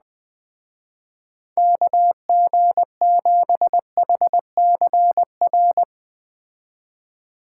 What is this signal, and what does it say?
Morse code, heard as a keyed tone, about 20 wpm, 700 Hz. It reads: KG7HCR